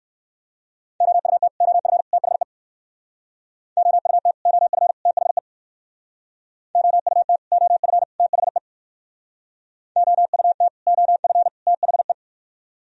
Morse code, heard as a keyed tone, about 55 wpm, 700 Hz. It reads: OUT OF THE OUT OF THE OUT OF THE OUT OF THE
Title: Speed-Racing - Sets of 3 Words 45wpm